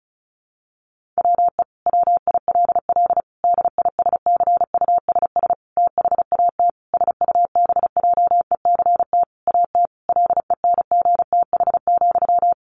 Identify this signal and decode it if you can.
Morse code, heard as a keyed tone, about 34 wpm, 700 Hz. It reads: WE WILL DISCUSS THAT SUBJECT AT LENGTH,
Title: Sentences from Top 1000 Words 35wpm